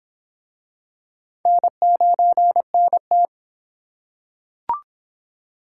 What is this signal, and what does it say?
Morse code, heard as a keyed tone, about 23 wpm, 700 Hz. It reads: N9NT